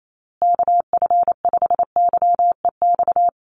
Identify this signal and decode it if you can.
Morse code, heard as a keyed tone, about 28 wpm, 700 Hz. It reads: KF5YEX
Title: USA Call Signs Repeated 3x 28wpm